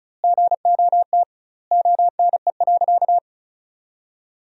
Morse code, heard as a keyed tone, about 34 wpm, 700 Hz. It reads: GOT ONE.
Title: 2x Word-Spacing - Sentences from Top 300 Words - 35wpm